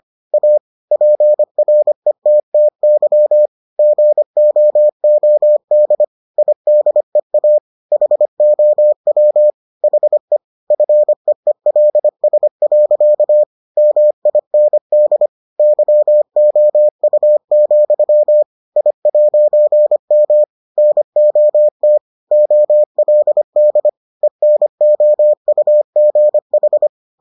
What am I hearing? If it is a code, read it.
Morse code, heard as a keyed tone, about 25 wpm, 600 Hz. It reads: A PRETTY GOOD IDEA HOW HE FEELS. MIND YOU, I'M NOT OLD ENOUGH